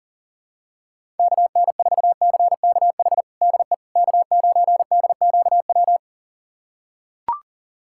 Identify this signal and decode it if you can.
Morse code, heard as a keyed tone, about 40 wpm, 700 Hz. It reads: KN4CKH DE K9DQW
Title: USA Callsign DE Callsign 40wpm